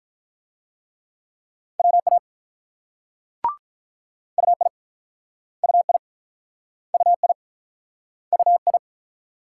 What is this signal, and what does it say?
Morse code, heard as a keyed tone, about 46 wpm, 700 Hz. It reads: WA UI UI UI UI